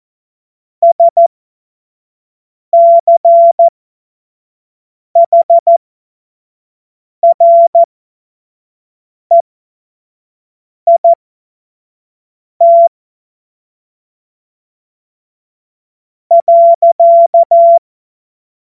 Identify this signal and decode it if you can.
Morse code, heard as a keyed tone, about 14 wpm, spaced as for 4 wpm, 680 Hz. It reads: SCHREIT.
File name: Morsekurs / Witze / Witz 04. April 2023 25 F70